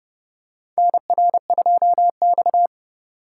Morse code, heard as a keyed tone, about 30 wpm, 700 Hz. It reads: NR2X
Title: USA Call Signs - 4 Characters - 4x Character Spacing - 30wpm